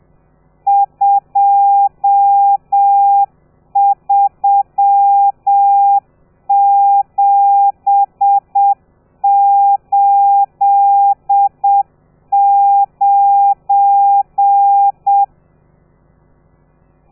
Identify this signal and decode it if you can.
Morse code, heard as a keyed tone, about 7 wpm, 800 Hz. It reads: 23789
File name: telegrafia / i2tfj / Lez12 30